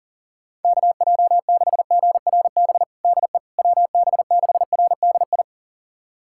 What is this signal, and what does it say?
Morse code, heard as a keyed tone, about 40 wpm, 700 Hz. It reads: KJ6GRB DE WB6RDI
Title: USA Callsign DE Callsign - 6 Characters - 40wpm